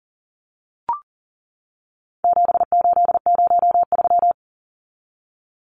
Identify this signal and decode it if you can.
Morse code, heard as a keyed tone, about 40 wpm, 700 Hz. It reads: 7803
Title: Numbers - 4-Digit 40wpm